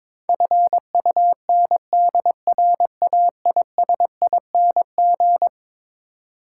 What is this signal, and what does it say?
Morse code, heard as a keyed tone, about 22 wpm, 700 Hz. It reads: FUNDRAISING
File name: practice / 22wpm / Eleven Letter Words - 22wpm